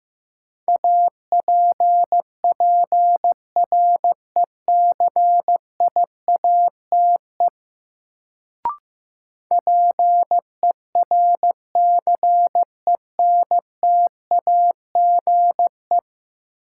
Morse code, heard as a keyed tone, about 15 wpm, 700 Hz. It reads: APPRECIATE PERCENTAGE
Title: Sentences from Top 1300 Words - New Word Drill - 15wpm